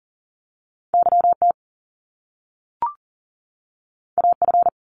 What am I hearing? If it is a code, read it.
Morse code, heard as a keyed tone, about 38 wpm, 700 Hz. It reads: YT AF
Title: Top Two Letter Combinations in Words 40wpm